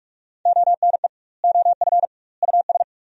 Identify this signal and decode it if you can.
Morse code, heard as a keyed tone, about 47 wpm, 700 Hz. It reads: ONE OF US
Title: 2x Word-Spacing - Sentences from Top 300 Words - 45wpm